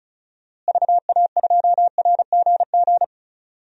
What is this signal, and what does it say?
Morse code, heard as a keyed tone, about 34 wpm, 700 Hz. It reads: VA2RGG
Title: DXCC-100 Call Signs - 6 Characters - Repeated 2x - 35wpm